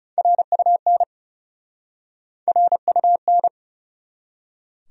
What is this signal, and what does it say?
Morse code, heard as a keyed tone, about 34 wpm, 700 Hz. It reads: RUN RUN
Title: Speed-Racing Top 500 Words 30wpm